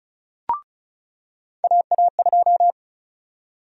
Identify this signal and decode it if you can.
Morse code, heard as a keyed tone, about 36 wpm, 700 Hz. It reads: AA2
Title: USA Call Sign Prefixes - 35wpm